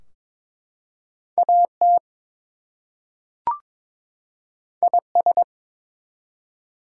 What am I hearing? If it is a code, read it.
Morse code, heard as a keyed tone, about 22 wpm, 700 Hz. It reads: AT IS
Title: Two Letter Words - 22wpm